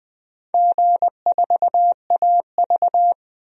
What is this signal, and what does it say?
Morse code, heard as a keyed tone, about 20 wpm, 700 Hz. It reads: G4AV
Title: DXCC-100 Call Signs - 4 Characters - 20wpm